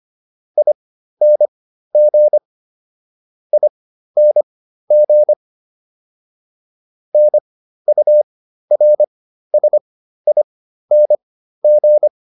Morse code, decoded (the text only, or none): ING ING NURSING